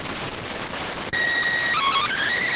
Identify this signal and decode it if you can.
SSTV (slow-scan television), PD 290.